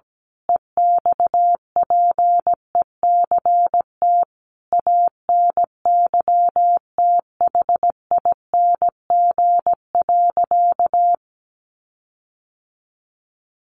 Morse code, heard as a keyed tone, about 17 wpm, 700 Hz. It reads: EXPECT ANYTHING.
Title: Sentences from Top 1400 Words 17wpm